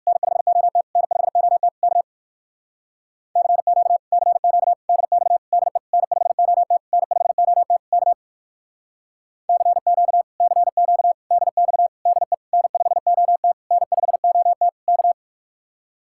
Morse code, heard as a keyed tone, about 65 wpm, 700 Hz. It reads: N5OT N5OT K CQ CQ DX DE N5OT N5OT K CQ CQ DX DE N5OT N5OT K